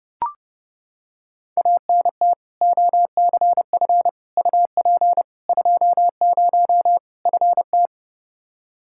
Morse code, heard as a keyed tone, about 32 wpm, 700 Hz. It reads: ANT OCF UP 20 FT
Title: Practice QSO - Exchange 2 - Info - 30wpm